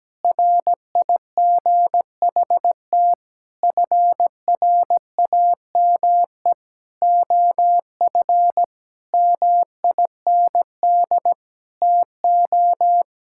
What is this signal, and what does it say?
Morse code, heard as a keyed tone, about 17 wpm, 700 Hz. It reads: RIGHT FRAME OF MIND TO
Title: Sentences from Top 1000 Words - Review Part 6 - 17wpm